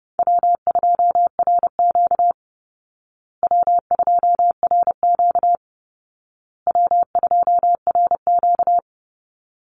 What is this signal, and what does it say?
Morse code, heard as a keyed tone, about 30 wpm, 700 Hz. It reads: W2RQ W2RQ W2RQ
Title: Easy Call Signs Repeated 3x - 30wpm